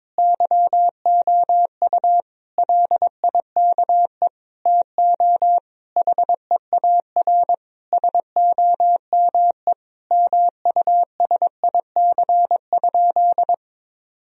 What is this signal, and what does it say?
Morse code, heard as a keyed tone, about 22 wpm, 700 Hz. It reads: YOU LIKE TO HEAR SOME MUSIC?
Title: Sentences from Top 1000 Words - Review Part 1 - 22wpm